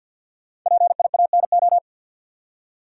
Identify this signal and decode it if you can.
Morse code, heard as a keyed tone, about 52 wpm, 700 Hz. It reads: PIANO
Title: Five Letter Words - 50wpm